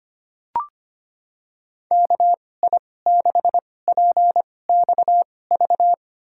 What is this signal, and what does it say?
Morse code, heard as a keyed tone, about 25 wpm, spaced as for 18 wpm, 700 Hz. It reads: KI6PXV
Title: USA Call Signs - 6 Characters - 2x Character Spacing - 25wpm